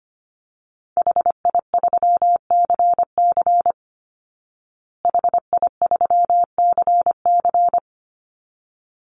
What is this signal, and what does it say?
Morse code, heard as a keyed tone, about 25 wpm, 700 Hz. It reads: HI3CC HI3CC
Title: Call Signs Repeated 3x - 25wpm